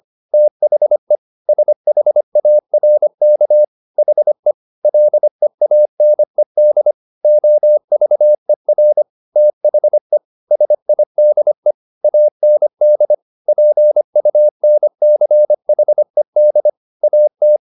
Morse code, heard as a keyed tone, about 25 wpm, 600 Hz. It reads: THE SHARK HE LEANED OVER THE SIDE AND PUNCHED AT